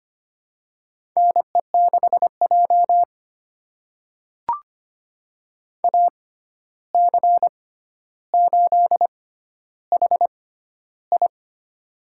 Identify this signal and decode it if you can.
Morse code, heard as a keyed tone, about 25 wpm, spaced as for 12 wpm, 700 Hz. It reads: NE6J AC8HI
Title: USA Call Signs - 6x Character Spacing - 25wpm